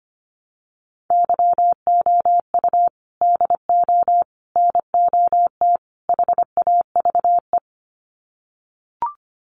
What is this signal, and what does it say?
Morse code, heard as a keyed tone, about 25 wpm, 700 Hz. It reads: YOU DO NOT HAVE
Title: Sets of 4 Words 25wpm